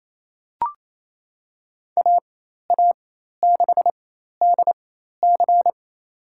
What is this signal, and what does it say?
Morse code, heard as a keyed tone, about 28 wpm, spaced as for 13 wpm, 700 Hz. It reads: AA6DC
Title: USA Call Signs - 5 Characters - 4x Character Spacing - 28wpm